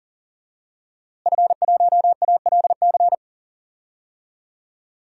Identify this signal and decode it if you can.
Morse code, heard as a keyed tone, about 40 wpm, 700 Hz. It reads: F1ALC
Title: DXCC-100 Call Signs - 5 Characters - Repeated 3x - 40wpm